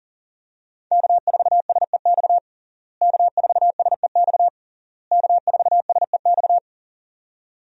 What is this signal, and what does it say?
Morse code, heard as a keyed tone, about 37 wpm, 700 Hz. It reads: K4SEX K4SEX K4SEX
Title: USA Call Signs - 5 Characters - Repeated 3x - 40wpm